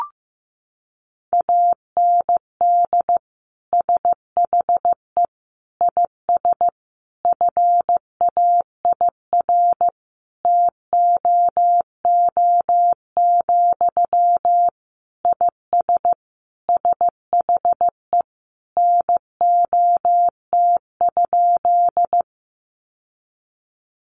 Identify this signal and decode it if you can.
Morse code, heard as a keyed tone, about 15 wpm, 700 Hz. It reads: AND SHE IS FAIR TOO, IS SHE NOT?
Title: Sentences from Top 1300 Words 15wpm